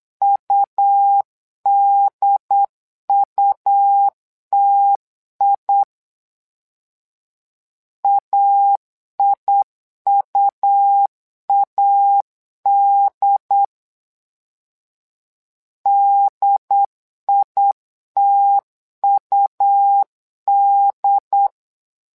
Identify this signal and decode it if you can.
Morse code, heard as a keyed tone, about 8 wpm, 800 Hz. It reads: UDUTI AIUAD DITUD
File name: LEZIONI / 2 UHDN / 2 UHD 8wpm 2258